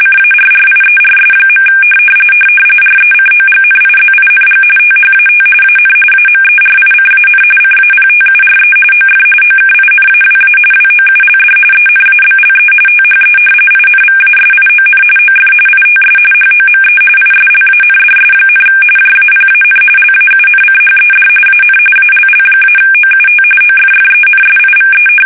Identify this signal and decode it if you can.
RTTY, about 45.45 baud. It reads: CARRIERS. RECOGNIZING THAT AN AMERICAN CAPTURE OF OKINAWA WOULD ENABLE THE INVASION OF ITS HOME ISLANDS, JAPAN LAUNCHED AN ALL-OUT OFFENSIVE
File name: fsk850 / battleshipnj